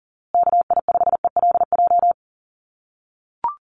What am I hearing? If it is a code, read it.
Morse code, heard as a keyed tone, about 40 wpm, 700 Hz. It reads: KI5ELJ
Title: USA Call Signs Repeated 2x 40wpm